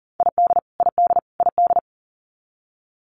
Morse code, heard as a keyed tone, about 40 wpm, 700 Hz. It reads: ID ID ID
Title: US States Repeated 3x 40wpm